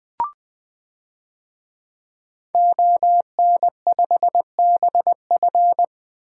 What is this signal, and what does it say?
Morse code, heard as a keyed tone, about 20 wpm, 700 Hz. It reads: ON5BF
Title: DXCC-100 Call Signs - 5 Characters - 20wpm